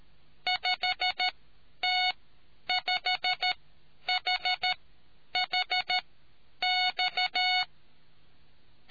Morse code, heard as a keyed tone, about 13 wpm, spaced as for 9 wpm, 730 Hz. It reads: 5T5HHX